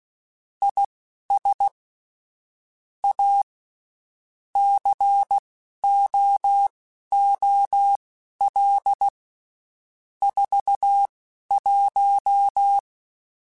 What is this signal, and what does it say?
Morse code, heard as a keyed tone, about 16 wpm, spaced as for 11 wpm, 800 Hz. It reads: IS A COOL 41
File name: Spaced Characters / Spaced Characters-mp3s / Weather 4 14 sp @15